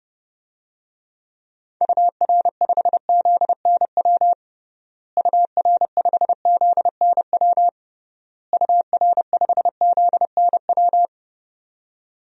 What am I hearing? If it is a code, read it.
Morse code, heard as a keyed tone, about 32 wpm, 700 Hz. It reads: UR5ZNW UR5ZNW UR5ZNW
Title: DXCC-100 Call Signs - 6 Characters - Repeated 3x - 30wpm